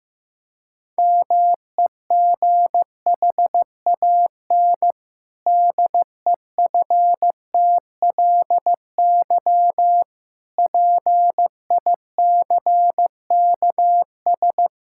Morse code, heard as a keyed tone, about 15 wpm, 700 Hz. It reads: MEGHAN DEFTLY PICKS
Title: Pangram Challenge v2 - 15wpm